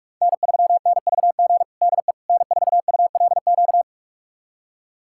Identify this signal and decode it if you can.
Morse code, heard as a keyed tone, about 46 wpm, 700 Hz. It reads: N3NVG DE N4ULQ
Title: USA Callsign DE Callsign - 5 Characters - 45wpm